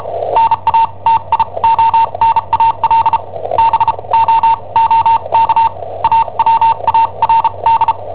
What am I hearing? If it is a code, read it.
Morse code, heard as a keyed tone, about 34 wpm, spaced as for 28 wpm, 910 Hz. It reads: NATIONAL BOOK AWARD
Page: Cassetta esercizi preparata da I7OHP per IW5ELP